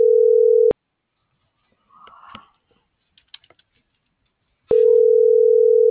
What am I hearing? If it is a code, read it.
Ringback tone (North America), 440 + 480 Hz.